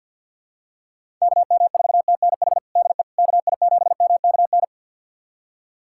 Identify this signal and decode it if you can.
Morse code, heard as a keyed tone, about 47 wpm, 700 Hz. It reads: KM4TNH DE KI7MKN
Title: USA Callsign DE Callsign - 6 Characters - 50wpm